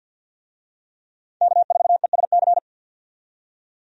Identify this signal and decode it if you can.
Morse code, heard as a keyed tone, about 52 wpm, 700 Hz. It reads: K4ESC
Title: USA Call Signs - 5 Characters - 50wpm